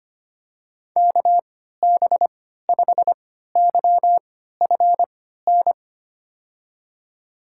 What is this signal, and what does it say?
Morse code, heard as a keyed tone, about 25 wpm, spaced as for 14 wpm, 700 Hz. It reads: KB5YFN
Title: USA Call Signs - 6 Characters - 3x Character Spacing - 25wpm